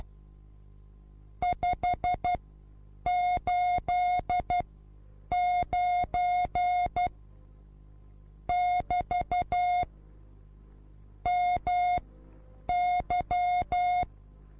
Morse code, heard as a keyed tone, about 12 wpm, spaced as for 8 wpm, 720 Hz. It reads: 589 <BT> MY
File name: repos / Morsekurs / DJ4UF Morsen24